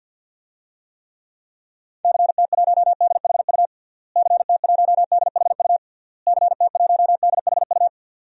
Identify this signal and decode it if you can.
Morse code, heard as a keyed tone, about 51 wpm, 700 Hz. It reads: CT1DHU CT1DHU CT1DHU
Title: DXCC-100 Call Signs - 6 Characters - Repeated 3x - 50wpm